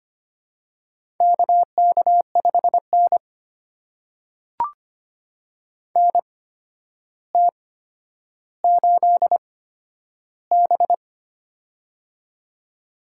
Morse code, heard as a keyed tone, about 25 wpm, spaced as for 11 wpm, 700 Hz. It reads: KK5N NT8B
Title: USA Call Signs - 4 Characters - 8x Character Spacing - 25wpm